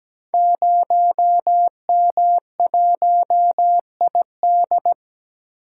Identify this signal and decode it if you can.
Morse code, heard as a keyed tone, about 17 wpm, 700 Hz. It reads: 0M1ID
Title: Code Groups - 5 Letter:Numbers - 17wpm